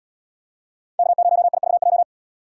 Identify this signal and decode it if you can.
Morse code, heard as a keyed tone, about 70 wpm, 700 Hz. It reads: K0IVJ